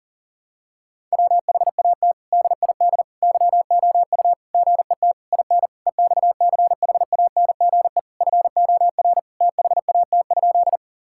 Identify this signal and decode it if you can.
Morse code, heard as a keyed tone, about 40 wpm, 700 Hz. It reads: WHAT DID YOU GET IN EXCHANGE FOR THAT?
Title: Sentences from Top 1000 Words - Review Part 6 - 40wpm